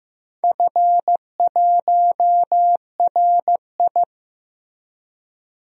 Morse code, heard as a keyed tone, about 15 wpm, 700 Hz. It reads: F1RI
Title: DXCC-100 Call Signs - 4 Characters - 15wpm